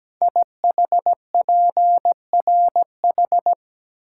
Morse code, heard as a keyed tone, about 17 wpm, 700 Hz. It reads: IHPRH
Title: Code Groups - 5 Letters - 17wpm